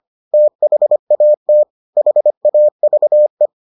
Morse code, heard as a keyed tone, about 25 wpm, 600 Hz. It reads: THAT HAVE